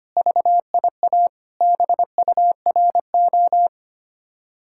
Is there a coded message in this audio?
Yes: Morse code from 0.2 to 3.7 s, heard as a keyed tone, about 32 wpm, 700 Hz. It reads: VIA BURO